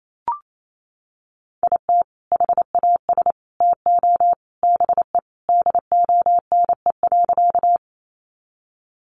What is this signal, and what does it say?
Morse code, heard as a keyed tone, about 28 wpm, 700 Hz. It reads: IT HAS TO BE DONE.
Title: Sentences from Top 500 Words 28wpm